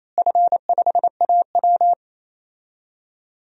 Morse code, heard as a keyed tone, about 28 wpm, 700 Hz. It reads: F5AW
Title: DXCC-100 Call Signs - 4 Characters - Repeated 3x - 28wpm